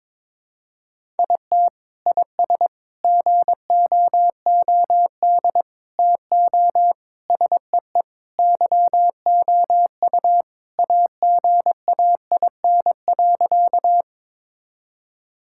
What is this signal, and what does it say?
Morse code, heard as a keyed tone, about 22 wpm, 700 Hz. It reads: IT IS GOOD TO SEE YOU AGAIN.
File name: practice / 22wpm / Sentences with 4-Word Phrases - 22wpm